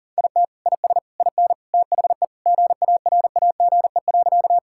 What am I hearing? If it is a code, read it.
Morse code, heard as a keyed tone, about 37 wpm, 700 Hz. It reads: IT IS IN THE GARAGE.
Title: Sentences from Top 1600 Words 40wpm